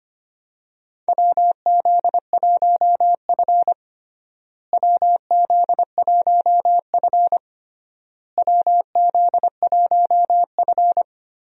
Morse code, heard as a keyed tone, about 25 wpm, 700 Hz. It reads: WZ1F WZ1F WZ1F
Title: USA Call Signs - 4 Characters - Repeated 3x - 25wpm